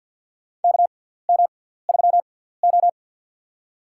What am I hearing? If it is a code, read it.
Morse code, heard as a keyed tone, about 53 wpm, spaced as for 17 wpm, 700 Hz. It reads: KM3O
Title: USA Call Signs - 4 Characters - 6x Character Spacing - 50wpm